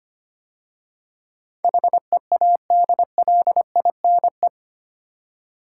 Morse code, heard as a keyed tone, about 25 wpm, 700 Hz. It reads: HEADLINE
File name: 8 Letter Words - First 3 Letters Given - 25wpm